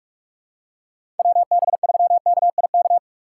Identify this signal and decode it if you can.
Morse code, heard as a keyed tone, about 46 wpm, 700 Hz. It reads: WB3KIK
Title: USA Call Signs - 2x Character Spacing - 45wpm